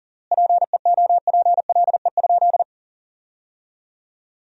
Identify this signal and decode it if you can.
Morse code, heard as a keyed tone, about 37 wpm, 700 Hz. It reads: PEOPLE?
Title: 3x Word-Spacing - Sentences from Top 200 Words - 40wpm